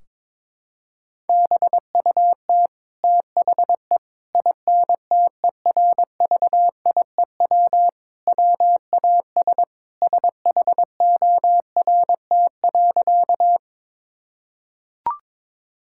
Morse code, heard as a keyed tone, about 22 wpm, 700 Hz. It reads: BUT THE INTERVIEW WAS SHORT.
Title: Sentences from Top 1600 Words 22wpm